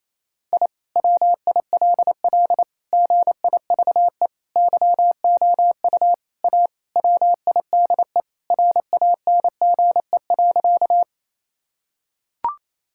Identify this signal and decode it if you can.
Morse code, heard as a keyed tone, about 28 wpm, 700 Hz. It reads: I WILL GIVE YOU A WIDE RANGE.
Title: Sentences from Top 1000 Words - Review Part 6 - 28wpm